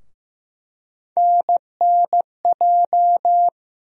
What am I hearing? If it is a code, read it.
Morse code, heard as a keyed tone, about 15 wpm, 700 Hz. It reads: NNJ
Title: USA Call Sign Postfixes - 3 Letters 15wpm